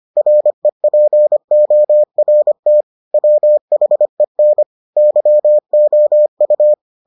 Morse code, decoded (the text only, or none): REPORT WHEN YOU